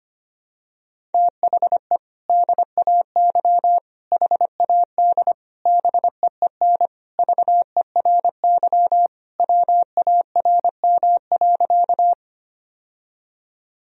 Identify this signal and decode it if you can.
Morse code, heard as a keyed tone, about 25 wpm, 700 Hz. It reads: THE DAY HAD BEEN VERY WARM.
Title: Sentences from Top 1000 Words - Review Part 2 - 25wpm